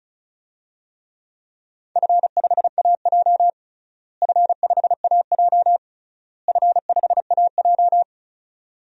Morse code, heard as a keyed tone, about 34 wpm, 700 Hz. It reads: F5AJ F5AJ F5AJ
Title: DXCC-100 Call Signs - 4 Characters - Repeated 3x - 35wpm